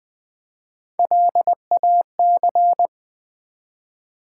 Morse code, heard as a keyed tone, about 20 wpm, 700 Hz. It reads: LAC
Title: Build A Word - 1 Letter At A Time - 3 to 5-Letter Words - 20wpm